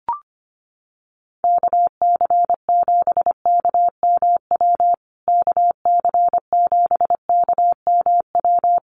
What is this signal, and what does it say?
Morse code, heard as a keyed tone, about 25 wpm, 700 Hz. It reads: KC7KMW KC7KMW
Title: USA Call Signs Repeated 3x 25wpm